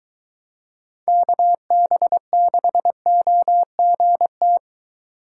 Morse code, heard as a keyed tone, about 21 wpm, 700 Hz. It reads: KB6OGT